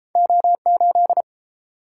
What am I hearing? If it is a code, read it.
Morse code, heard as a keyed tone, about 30 wpm, 700 Hz. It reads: O8